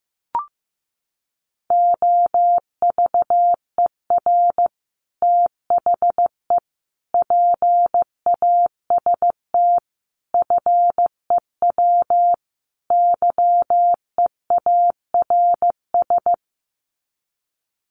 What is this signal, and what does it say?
Morse code, heard as a keyed tone, about 15 wpm, 700 Hz. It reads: OVER THE PAST FEW YEARS
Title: Sets of 5 Words Encore 15wpm